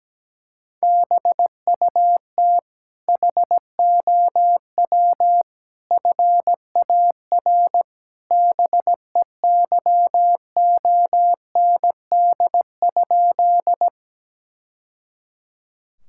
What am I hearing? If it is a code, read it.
Morse code, heard as a keyed tone, about 17 wpm, 700 Hz. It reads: BUT HOW FAR BEYOND?
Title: Sentences from Top 1000 Words - Review Part 6 - 17wpm